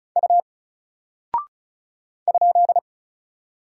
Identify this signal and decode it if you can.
Morse code, heard as a keyed tone, about 36 wpm, 700 Hz. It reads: U ?
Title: Single Character (Letter:Number:Punctuation:Prosign) Alt 35wpm